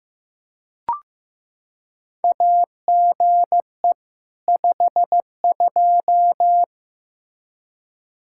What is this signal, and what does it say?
Morse code, heard as a keyed tone, about 15 wpm, 700 Hz. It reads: AGE 52
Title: Practice QSO - Exchange 3 - More Info - 15wpm